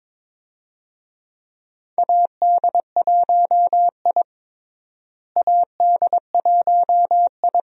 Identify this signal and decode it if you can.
Morse code, heard as a keyed tone, about 22 wpm, 700 Hz. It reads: AD1I AD1I
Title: USA Call Signs - 4 Characters - Repeated 2x - 22wpm